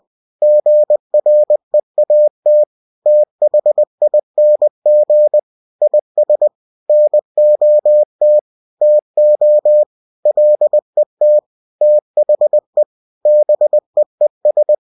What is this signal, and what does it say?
Morse code, heard as a keyed tone, about 20 wpm, 600 Hz. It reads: GREAT THING IS NOT TO LET THE BEES
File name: cw-books / winnie-the-pooh / winnie-20wpm-page-0005